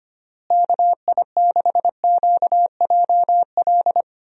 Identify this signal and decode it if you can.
Morse code, heard as a keyed tone, about 25 wpm, 700 Hz. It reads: KI6QJL
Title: USA Call Signs - 6 Characters - Repeated 2x - 25wpm